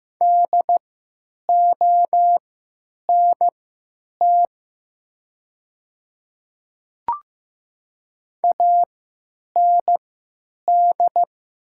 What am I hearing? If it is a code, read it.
Morse code, heard as a keyed tone, about 15 wpm, spaced as for 9 wpm, 700 Hz. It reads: DONT AND
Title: 3x Character Spacing - Top 300 Words - 15wpm